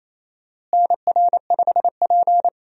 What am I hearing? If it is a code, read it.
Morse code, heard as a keyed tone, about 28 wpm, 700 Hz. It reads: NR5P
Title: USA Call Signs - 4 Characters - Repeated 3x - 28wpm